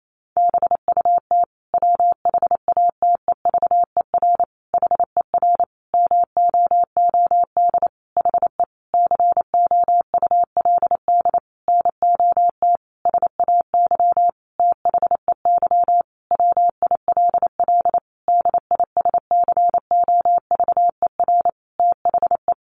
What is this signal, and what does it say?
Morse code, heard as a keyed tone, about 28 wpm, 700 Hz. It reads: BUT WHATEVER HER MOOD HE COULD NOT SAY THEY WILL DISCOVER THE
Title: Sentences from Top 1100 Words 28wpm